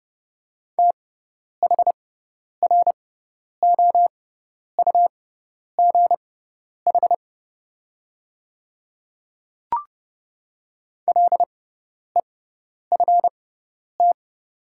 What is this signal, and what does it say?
Morse code, heard as a keyed tone, about 32 wpm, spaced as for 10 wpm, 700 Hz. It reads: THROUGH LEFT
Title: 6x Character Spacing - Top 300 Words - 30wpm